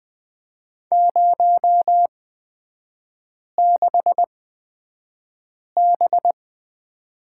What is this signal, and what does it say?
Morse code, heard as a keyed tone, about 20 wpm, spaced as for 5 wpm, 700 Hz. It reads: 06B
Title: 243 Lesson 39e - Letter z - Three Characters 20wpm